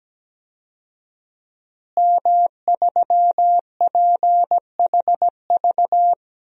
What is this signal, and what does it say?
Morse code, heard as a keyed tone, about 17 wpm, 700 Hz. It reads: M3PHV
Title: DXCC-100 Call Signs - 5 Characters - Repeated 2x - 17wpm